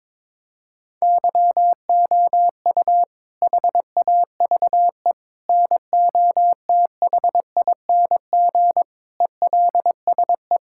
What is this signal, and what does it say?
Morse code, heard as a keyed tone, about 22 wpm, 700 Hz. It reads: YOU HAVE NOTHING ELSE
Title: Sentences from Top 1000 Words - Review Part 6 - 22wpm